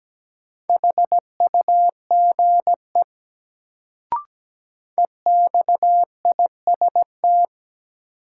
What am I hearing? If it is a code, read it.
Morse code, heard as a keyed tone, about 17 wpm, 700 Hz. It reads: HUGE EXIST
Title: Sentences from Top 1100 Words - New Word Drill Alt - 17wpm